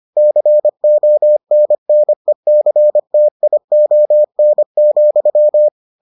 Morse code, heard as a keyed tone, about 25 wpm, 600 Hz. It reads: CONNECTION,